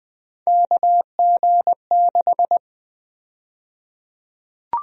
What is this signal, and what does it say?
Morse code, heard as a keyed tone, about 20 wpm, 700 Hz. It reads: KG6
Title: USA Call Sign Prefixes - 20wpm